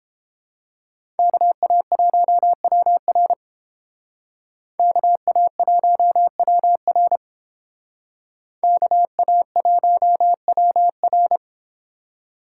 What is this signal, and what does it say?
Morse code, heard as a keyed tone, about 29 wpm, 700 Hz. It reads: KA1WR KA1WR KA1WR